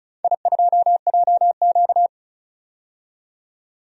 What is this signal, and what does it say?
Morse code, heard as a keyed tone, about 36 wpm, 700 Hz. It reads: I2JQ
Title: DXCC-100 Call Signs - 4 Characters - 35wpm